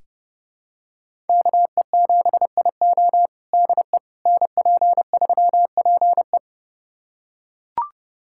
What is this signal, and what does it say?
Morse code, heard as a keyed tone, about 30 wpm, 700 Hz. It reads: KE7IO DE NP3PE
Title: USA Callsign DE Callsign - 5 Characters - 30wpm